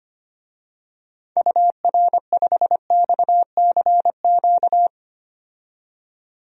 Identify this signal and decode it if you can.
Morse code, heard as a keyed tone, about 25 wpm, 700 Hz. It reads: UR5XCQ
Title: DXCC-100 Call Signs - 6 Characters - 25wpm